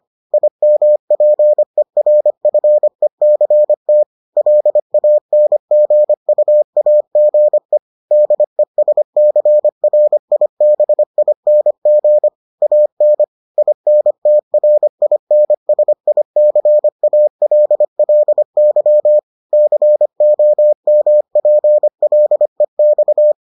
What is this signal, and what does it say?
Morse code, heard as a keyed tone, about 25 wpm, 600 Hz. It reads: IMPERFECT LANGUAGE DESCRIBING AN INTRINSICALLY COMPLEX